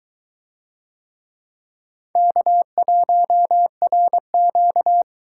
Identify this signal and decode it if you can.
Morse code, heard as a keyed tone, about 23 wpm, 700 Hz. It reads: K1RQ